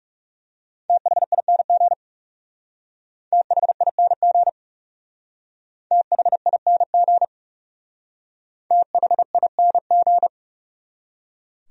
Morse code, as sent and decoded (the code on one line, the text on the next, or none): - .... .. -. --. / - .... .. -. --. / - .... .. -. --. / - .... .. -. --.
THING THING THING THING